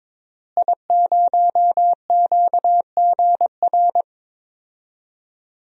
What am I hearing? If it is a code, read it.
Morse code, heard as a keyed tone, about 22 wpm, 700 Hz. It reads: I0QGR
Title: DXCC-100 Call Signs - 5 Characters - Repeated 2x - 22wpm